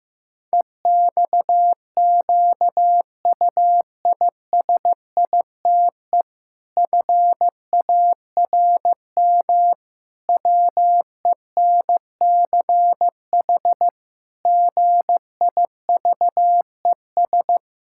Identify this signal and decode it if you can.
Morse code, heard as a keyed tone, about 15 wpm, 700 Hz. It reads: EXQUISITE FARM WENCH GIVES
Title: Pangram Challenge v2 - 15wpm